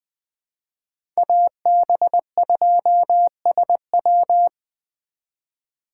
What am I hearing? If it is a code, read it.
Morse code, heard as a keyed tone, about 21 wpm, 700 Hz. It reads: AB2SW
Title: USA Call Signs - 3x Character Spacing - 20wpm